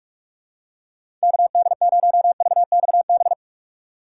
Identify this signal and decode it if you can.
Morse code, heard as a keyed tone, about 46 wpm, 700 Hz. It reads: KD0VXB
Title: USA Call Signs - 6 Characters - 4x Character Spacing - 45wpm